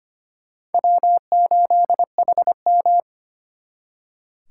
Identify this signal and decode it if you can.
Morse code, heard as a keyed tone, about 34 wpm, spaced as for 30 wpm, 700 Hz. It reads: W8HM
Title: Speed Racing - USA 1x2 Call Signs 25wpm